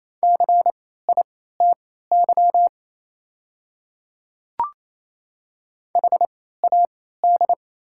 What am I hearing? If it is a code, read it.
Morse code, heard as a keyed tone, about 28 wpm, spaced as for 16 wpm, 700 Hz. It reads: CITY HAD
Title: 3x Character Spacing - Top 300 Words - 28wpm